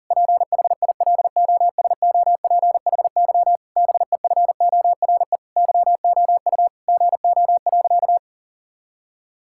Morse code, heard as a keyed tone, about 37 wpm, 700 Hz. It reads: PHILOSOPHY BEFORE YOU GO.
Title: Sentences from Top 1000 Words - Review Part 6 - 40wpm